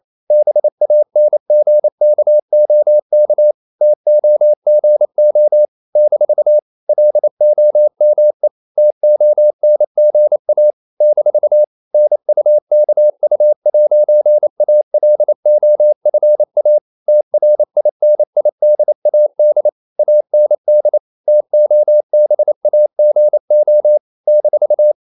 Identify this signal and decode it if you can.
Morse code, heard as a keyed tone, about 28 wpm, 600 Hz. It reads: BANGKOK TOGO - LOME TONGA - NUKU'ALOFA TRINIDAD AND TOBAGO -
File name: CW / Practice / NationalCapitals28